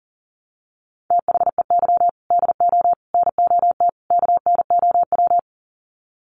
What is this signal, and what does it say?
Morse code, heard as a keyed tone, about 40 wpm, 700 Hz. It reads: THEY DO NOT KNOW
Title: Sets of 4 Words 40wpm